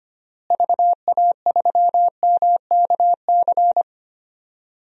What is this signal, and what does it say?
Morse code, heard as a keyed tone, about 25 wpm, 700 Hz. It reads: VA3MKC
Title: DXCC-100 Call Signs - 6 Characters - Repeated 3x - 25wpm